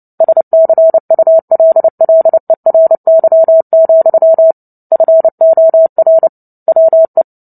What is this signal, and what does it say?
Morse code, heard as a keyed tone, about 30 wpm, 650 Hz. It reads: SCULLERY, FOR WE
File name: Book Two / 03-Chapter 3 - 29 WPM